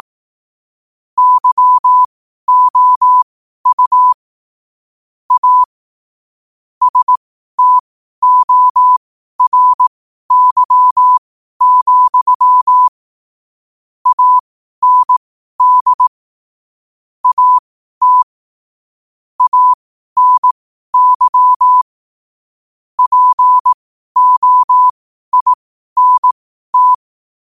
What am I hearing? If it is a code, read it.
Morse code, heard as a keyed tone, about 18 wpm, spaced as for 12 wpm, 1000 Hz. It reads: YOU A STORY, AND AT ANY POINT
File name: qotdfiles / 12wpm